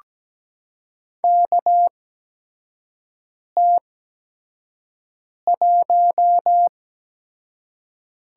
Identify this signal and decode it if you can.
Morse code, heard as a keyed tone, about 17 wpm, spaced as for 5 wpm, 700 Hz. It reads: KT1
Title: USA Call Signs - 8x Character Spacing - 17wpm